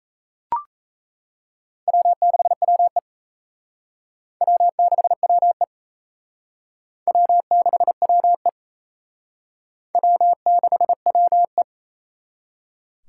Morse code, heard as a keyed tone, about 34 wpm, 700 Hz. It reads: W6WE W6WE W6WE W6WE
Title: Speed Racing - USA Call Signs 28wpm